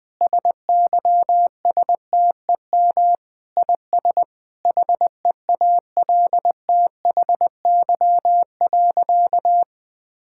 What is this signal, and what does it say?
Morse code, heard as a keyed tone, about 20 wpm, 700 Hz. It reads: SYSTEM IS HEALTHY.
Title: Sentences from Top 1000 Words - Review Part 5 - 20wpm